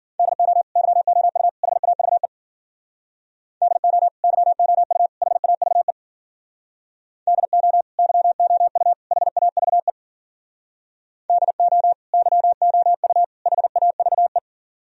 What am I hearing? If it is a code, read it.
Morse code, heard as a keyed tone, about 52 wpm, 700 Hz. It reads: DO YOU HAVE DO YOU HAVE DO YOU HAVE DO YOU HAVE